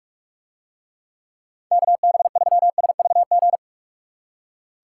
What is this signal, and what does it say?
Morse code, heard as a keyed tone, about 44 wpm, 700 Hz. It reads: KB3SVG